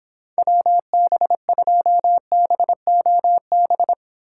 Morse code, heard as a keyed tone, about 30 wpm, 700 Hz. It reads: WB2BOB